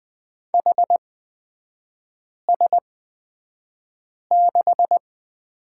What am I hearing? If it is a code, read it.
Morse code, heard as a keyed tone, about 20 wpm, spaced as for 5 wpm, 700 Hz. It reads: HS6
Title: 255 Lesson 41e - Period - Three Characters 20wpm